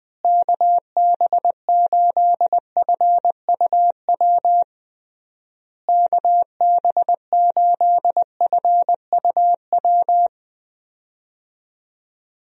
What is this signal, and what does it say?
Morse code, heard as a keyed tone, about 21 wpm, 700 Hz. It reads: KB8FUW KB8FUW
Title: USA Call Signs - 6 Characters - Repeated 3x - 20wpm